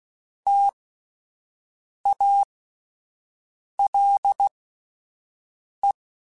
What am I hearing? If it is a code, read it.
Morse code, heard as a keyed tone, about 16 wpm, spaced as for 5 wpm, 800 Hz. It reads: TALE
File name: Alphabetical / Alphabetical mp3s / Ts @5